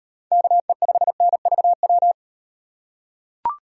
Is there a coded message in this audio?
Yes: Morse code from 0.3 to 2.1 s, heard as a keyed tone, about 38 wpm, 700 Hz. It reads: KE5NVW